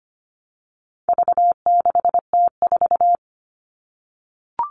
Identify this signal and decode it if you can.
Morse code, heard as a keyed tone, about 25 wpm, 700 Hz. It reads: V6T4
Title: All Cut Numbers - 4-Digit 25wpm